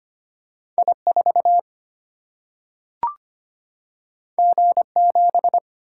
Morse code, heard as a keyed tone, about 25 wpm, 700 Hz. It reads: I 4 G 7
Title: Single-Letter Single-Number 25wpm